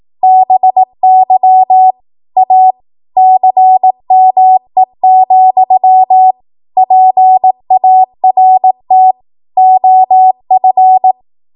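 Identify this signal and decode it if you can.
Morse code, heard as a keyed tone, about 18 wpm, 750 Hz. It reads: BY A CME, PART OF
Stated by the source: 18 WPM Morse code practice from a national amateur radio association's code archive.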